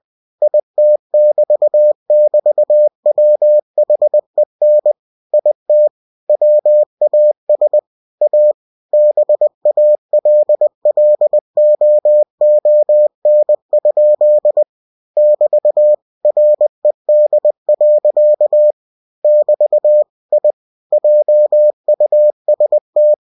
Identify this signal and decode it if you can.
Morse code, heard as a keyed tone, about 20 wpm, 600 Hz. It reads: IT <BT> <BT> WHEN IT WAS A BALLOON? <BT> RED. <BT> I JUST
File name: cw-books / winnie-the-pooh / winnie-20wpm-page-0056